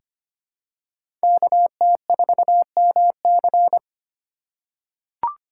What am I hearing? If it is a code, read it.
Morse code, heard as a keyed tone, about 25 wpm, 700 Hz. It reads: KT4MC
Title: USA Call Signs - 5 Characters - 2x Character Spacing - 25wpm